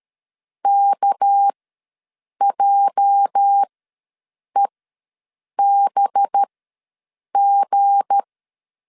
Morse code, heard as a keyed tone, about 13 wpm, spaced as for 7 wpm, 800 Hz. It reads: KJEBG
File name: LEZIONI / 8 YJ45 / 8 YJ ws 12wpm 3342